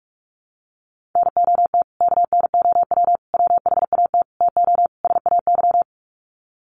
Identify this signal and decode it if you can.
Morse code, heard as a keyed tone, about 44 wpm, 700 Hz. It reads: NOT KNOW WHAT TO SAY
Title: Sets of 5 Words 45wpm